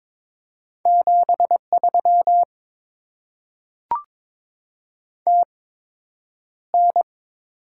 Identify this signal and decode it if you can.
Morse code, heard as a keyed tone, about 22 wpm, spaced as for 9 wpm, 700 Hz. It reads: 73 TN
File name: Top 20 QSO Elements - 8x Character Spacing - 22wpm